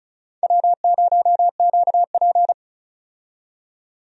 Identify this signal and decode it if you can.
Morse code, heard as a keyed tone, about 34 wpm, 700 Hz. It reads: W0QP
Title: USA Call Signs - 4 Characters - 35wpm